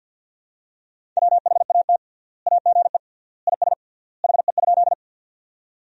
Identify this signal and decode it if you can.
Morse code, heard as a keyed tone, about 53 wpm, 700 Hz. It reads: WHAT AGE IS HE?
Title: 3x Word-Spacing - Sentences from Top 500 Words - 50wpm